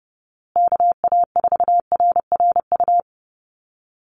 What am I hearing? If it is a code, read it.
Morse code, heard as a keyed tone, about 30 wpm, 700 Hz. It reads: KA4RRU
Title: Call Signs Repeated 3x - 30wpm